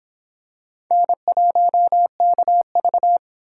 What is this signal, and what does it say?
Morse code, heard as a keyed tone, about 23 wpm, 700 Hz. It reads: N1KV